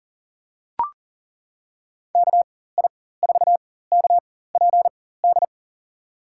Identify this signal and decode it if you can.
Morse code, heard as a keyed tone, about 40 wpm, spaced as for 19 wpm, 700 Hz. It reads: KI4KPD
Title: USA Call Signs - 6 Characters - 4x Character Spacing - 40wpm